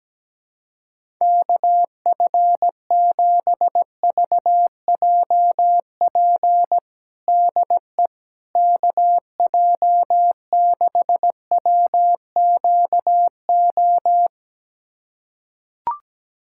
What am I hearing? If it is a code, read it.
Morse code, heard as a keyed tone, about 17 wpm, 700 Hz. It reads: KF7VJP DE KJ6WQO
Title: USA Callsign DE Callsign 17wpm